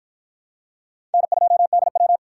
Morse code, heard as a keyed tone, about 53 wpm, 700 Hz. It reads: N2DW